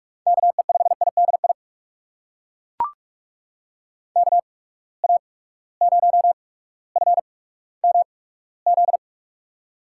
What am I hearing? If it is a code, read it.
Morse code, heard as a keyed tone, about 42 wpm, spaced as for 19 wpm, 700 Hz. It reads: KE5IDI KA0FMZ
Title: USA Call Signs - 6 Characters - 8x Character Spacing - 45wpm